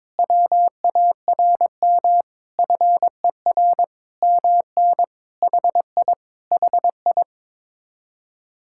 Morse code, heard as a keyed tone, about 22 wpm, 700 Hz. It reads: WARM FER MN HI HI
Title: Speed-Racing Common QSO Phrases - 22wpm